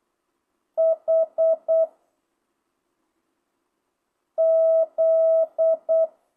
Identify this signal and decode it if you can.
Morse code, heard as a keyed tone, about 8 wpm, 660 Hz. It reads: H Z
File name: jh8cbh / 2025cw